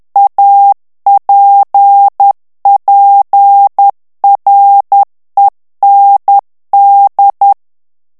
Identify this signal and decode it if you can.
Morse code, heard as a keyed tone, about 10 wpm, 800 Hz. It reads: APPREND
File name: FRANCAIS TEXTE CLAIR 2 VITESSE 10 MOTS MINUTES